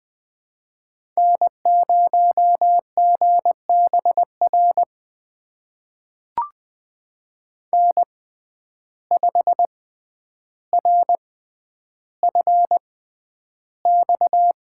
Morse code, heard as a keyed tone, about 20 wpm, spaced as for 10 wpm, 700 Hz. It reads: N0GBR N5RFX
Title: USA Call Signs - 5 Characters - 6x Character Spacing - 20wpm